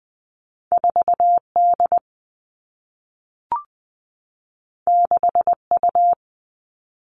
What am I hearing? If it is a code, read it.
Morse code, heard as a keyed tone, about 20 wpm, 700 Hz. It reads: 4 D 6 U
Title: Two Challenging Letter-Number Combinations 20wpm